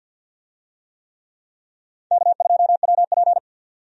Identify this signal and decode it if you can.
Morse code, heard as a keyed tone, about 48 wpm, 700 Hz. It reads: K2WP
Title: USA Call Signs - 4 Characters - 50wpm